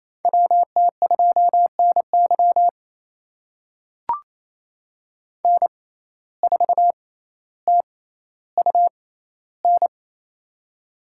Morse code, heard as a keyed tone, about 28 wpm, spaced as for 14 wpm, 700 Hz. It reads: WT2NY N4TUN
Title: USA Call Signs - 5 Characters - 6x Character Spacing - 28wpm